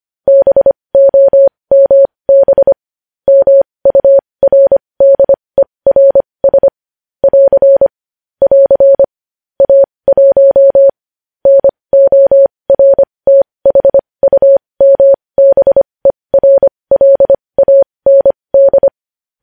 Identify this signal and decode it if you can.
Morse code, heard as a keyed tone, about 25 wpm, spaced as for 20 wpm, 550 Hz. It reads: BOMB MURDERS <AR> <AR> A1 NORTHUMBERLAND
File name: podcast / News headlines-1x-25x20